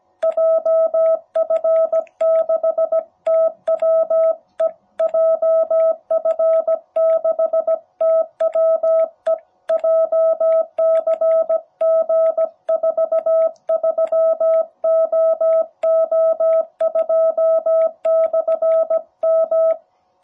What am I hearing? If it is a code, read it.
Morse code, heard as a keyed tone, about 17 wpm, 660 Hz. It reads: JF6TWE JF6TWE JCG43OO2/M
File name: jh8cbh / 2021cw